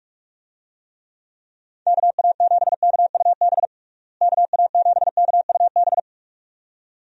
Morse code, heard as a keyed tone, about 44 wpm, 700 Hz. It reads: KA7KUB KA7KUB
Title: USA Call Signs - 6 Characters - Repeated 2x - 45wpm